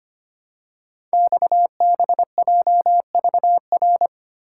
Morse code, heard as a keyed tone, about 25 wpm, 700 Hz. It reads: XBJVR
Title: Code Groups - 5 Letters - 25wpm